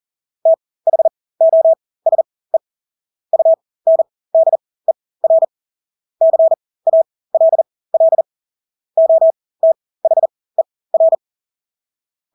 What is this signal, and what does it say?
Morse code, heard as a keyed tone, about 40 wpm, spaced as for 20 wpm, 650 Hz. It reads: THOSE UNDER CALL OTHER
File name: Engelske ord 40-20 wpm